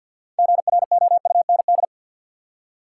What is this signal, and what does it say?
Morse code, heard as a keyed tone, about 53 wpm, 700 Hz. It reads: GROUND
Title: Six Letter Words - 50wpm